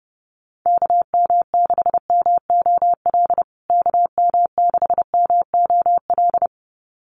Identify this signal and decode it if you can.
Morse code, heard as a keyed tone, about 30 wpm, 700 Hz. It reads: KM6MOL KM6MOL
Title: USA Call Signs Repeated 2x 30wpm